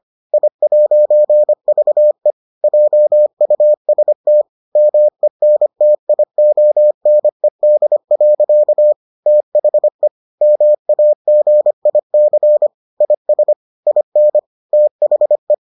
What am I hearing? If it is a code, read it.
Morse code, heard as a keyed tone, about 25 wpm, 600 Hz. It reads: I'VE JUST MENTIONED. THE MAGIC IS IN THE